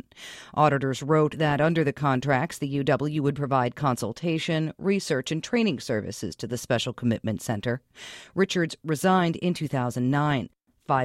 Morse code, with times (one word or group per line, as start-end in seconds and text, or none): none